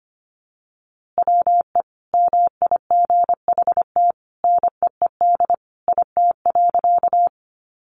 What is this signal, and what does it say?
Morse code, heard as a keyed tone, about 25 wpm, 700 Hz. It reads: WE MIGHT NEED IT.
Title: A Sound Of Thunder 25wpm